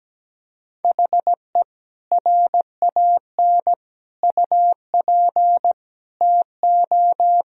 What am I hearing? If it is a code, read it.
Morse code, heard as a keyed tone, about 17 wpm, 700 Hz. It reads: HE RAN UP TO
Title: Sentences from Top 1000 Words - Review Part 3 - 17wpm